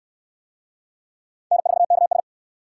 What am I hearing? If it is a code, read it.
Morse code, heard as a keyed tone, about 60 wpm, 700 Hz. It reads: N4KS